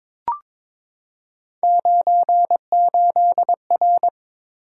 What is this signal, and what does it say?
Morse code, heard as a keyed tone, about 23 wpm, 700 Hz. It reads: 98R
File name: Three Letter-Number Combinations - 22wpm